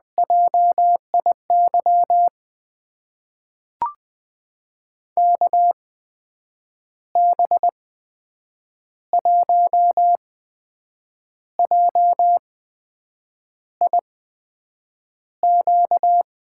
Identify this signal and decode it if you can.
Morse code, heard as a keyed tone, about 20 wpm, spaced as for 7 wpm, 700 Hz. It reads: JIY KB1JIQ
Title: USA Call Signs - 6 Characters - 8x Character Spacing - 20wpm